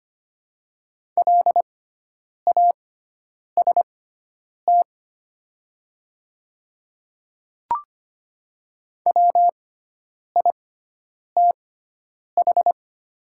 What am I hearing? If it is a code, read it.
Morse code, heard as a keyed tone, about 25 wpm, spaced as for 9 wpm, 700 Hz. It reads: LAST WITH
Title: 6x Character Spacing - Top 300 Words - 25wpm